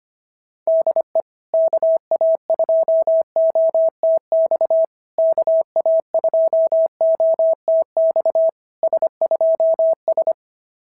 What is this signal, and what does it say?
Morse code, heard as a keyed tone, about 24 wpm, 650 Hz. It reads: DE KA2OTX KA2OTX S2S
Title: SOTA QSO Practice 25wpm